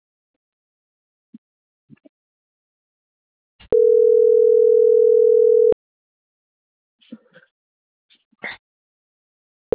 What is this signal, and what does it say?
Ringback tone (North America), 440 + 480 Hz.